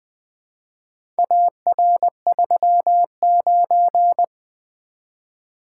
Morse code, heard as a keyed tone, about 20 wpm, 700 Hz. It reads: AR39
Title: Four Letter-Number Combinations - 20wpm